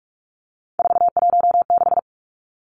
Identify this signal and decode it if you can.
Morse code, heard as a keyed tone, about 44 wpm, 700 Hz. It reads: 416
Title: Numbers - 3-Digit 45wpm